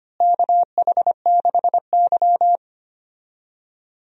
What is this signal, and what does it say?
Morse code, heard as a keyed tone, about 25 wpm, 700 Hz. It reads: KH6Y
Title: DXCC-100 Call Signs - 4 Characters - Repeated 2x - 25wpm